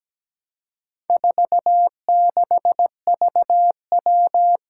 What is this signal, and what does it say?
Morse code, heard as a keyed tone, about 17 wpm, 700 Hz. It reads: 46VW
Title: Four Challenging Letter-Number Combinations 17wpm